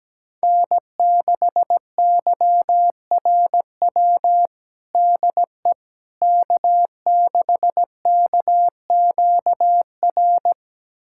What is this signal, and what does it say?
Morse code, heard as a keyed tone, about 17 wpm, 700 Hz. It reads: N6YRW DE K6KQR
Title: USA Callsign DE Callsign - 5 Characters - 17wpm